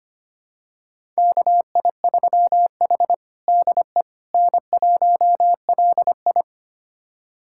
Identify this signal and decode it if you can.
Morse code, heard as a keyed tone, about 25 wpm, 700 Hz. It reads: KI3H DE N1LI
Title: USA Callsign DE Callsign - 4 Characters - 25wpm